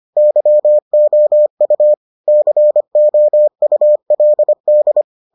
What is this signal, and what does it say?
Morse code, heard as a keyed tone, about 25 wpm, 600 Hz. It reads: YOU COULD